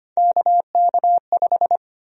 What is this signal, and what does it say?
Morse code, heard as a keyed tone, about 25 wpm, 700 Hz. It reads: KK5
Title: USA Call Sign Prefixes - Repeated 2x - 25wpm